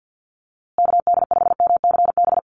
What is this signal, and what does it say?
Morse code, heard as a keyed tone, about 50 wpm, 700 Hz. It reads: KD5MCB
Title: USA Call Signs Repeated 3x 50wpm